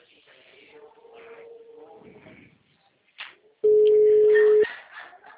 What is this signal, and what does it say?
Ringback tone (Europe, 425 Hz).